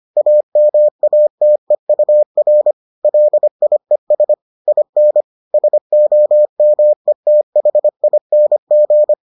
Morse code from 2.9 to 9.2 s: LIES IN SOMETHING

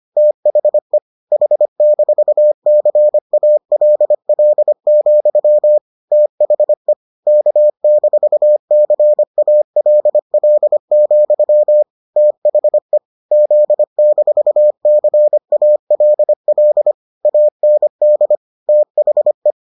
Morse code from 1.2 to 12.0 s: H-CALL, THE K-CALL,

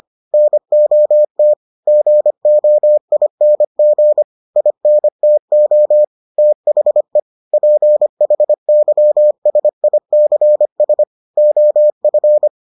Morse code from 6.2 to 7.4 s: THE

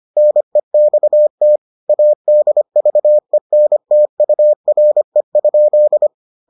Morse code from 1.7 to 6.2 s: ADVENTURE?

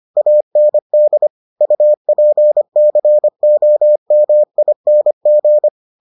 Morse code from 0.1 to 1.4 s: AND

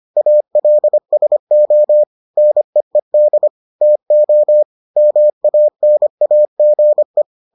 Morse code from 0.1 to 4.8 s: ALSO NEED TO

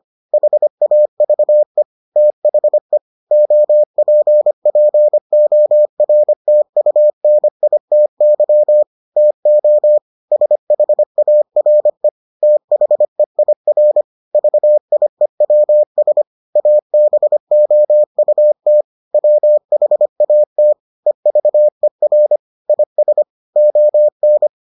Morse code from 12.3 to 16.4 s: THEIR VIEWS

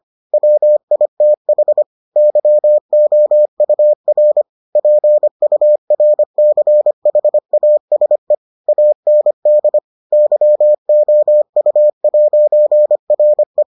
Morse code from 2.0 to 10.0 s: YOUR PURCHASE AND

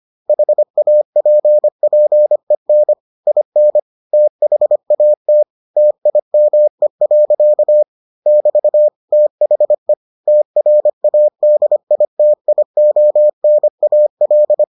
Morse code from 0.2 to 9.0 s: HAPPEN IN THAT TIME. <BT>

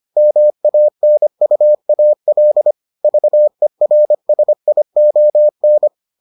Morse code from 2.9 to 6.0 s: VERSION

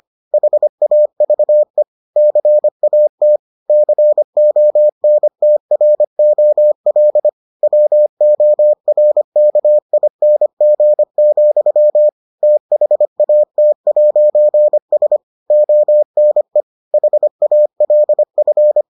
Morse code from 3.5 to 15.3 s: CONTROL WORKING, THAT'S

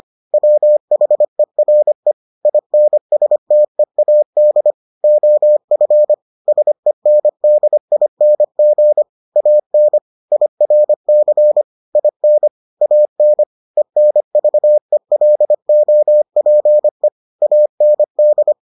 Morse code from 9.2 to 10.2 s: AN

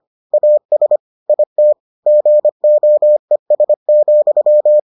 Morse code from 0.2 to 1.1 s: AS

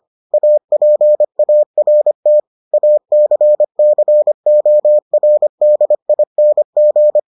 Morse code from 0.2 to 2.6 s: APART

